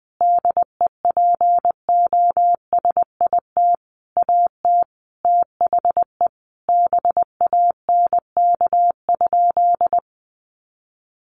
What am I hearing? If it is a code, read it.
Morse code, heard as a keyed tone, about 20 wpm, 700 Hz. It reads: DEPOSIT AT THE BANK?
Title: Sentences from Top 1300 Words 20wpm